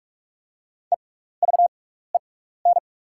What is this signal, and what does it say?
Morse code, heard as a keyed tone, about 48 wpm, spaced as for 16 wpm, 700 Hz. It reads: EVEN
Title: 6x Character Spacing - Top 500 Words - 45wpm